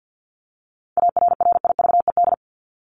Morse code, heard as a keyed tone, about 50 wpm, 700 Hz. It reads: ARRIVED
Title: ED Suffix - Top 100 Words with Base Words 50wpm